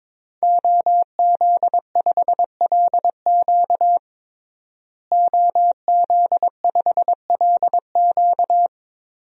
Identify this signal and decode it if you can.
Morse code, heard as a keyed tone, about 22 wpm, 700 Hz. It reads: OZ5LQ OZ5LQ
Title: DXCC-100 Call Signs - 5 Characters - Repeated 3x - 22wpm